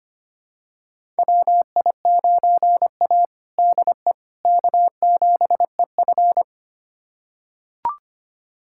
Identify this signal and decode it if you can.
Morse code, heard as a keyed tone, about 25 wpm, 700 Hz. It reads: WI9A DE K7EF
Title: USA Callsign DE Callsign 25wpm